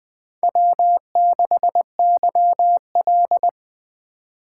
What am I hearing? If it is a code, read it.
Morse code, heard as a keyed tone, about 20 wpm, 700 Hz. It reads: W6YL
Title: USA Call Signs - 4 Characters - 2x Character Spacing - 20wpm